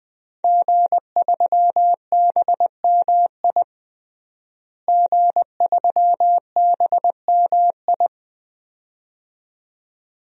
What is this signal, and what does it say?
Morse code, heard as a keyed tone, about 20 wpm, 700 Hz. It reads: G3BMI G3BMI
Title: DXCC-100 Call Signs - 5 Characters - Repeated 2x - 20wpm